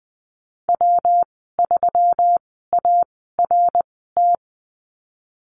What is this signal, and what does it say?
Morse code, heard as a keyed tone, about 20 wpm, spaced as for 15 wpm, 700 Hz. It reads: W3ART
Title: 154 Lesson 25g - Letter p - Call Signs 20wpm